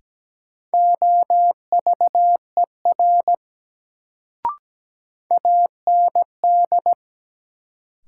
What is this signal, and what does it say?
Morse code, heard as a keyed tone, about 17 wpm, 700 Hz. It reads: OVER AND
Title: Sentences from Top 100 Words - New Word Drill - 17wpm